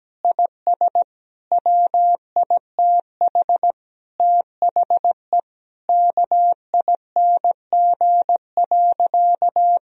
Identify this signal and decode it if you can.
Morse code, heard as a keyed tone, about 17 wpm, 700 Hz. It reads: IS WITH THE KING.
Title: Sentences from Top 1000 Words - Review Part 3 - 17wpm